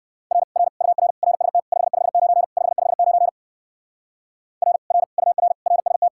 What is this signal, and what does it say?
Morse code, heard as a keyed tone, about 54 wpm, 700 Hz. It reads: R R UR RST 559 559 R R UR RST
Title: Speed-Racing Common QSO Phrases - 45wpm